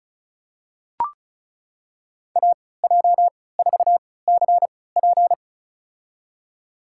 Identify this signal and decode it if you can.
Morse code, heard as a keyed tone, about 36 wpm, spaced as for 20 wpm, 700 Hz. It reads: AJ4CP
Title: USA Call Signs - 3x Character Spacing - 35wpm